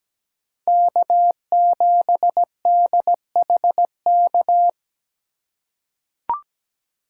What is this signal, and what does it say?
Morse code, heard as a keyed tone, about 17 wpm, 700 Hz. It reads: K7DHK
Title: USA Call Signs - 5 Characters - 2x Character Spacing - 17wpm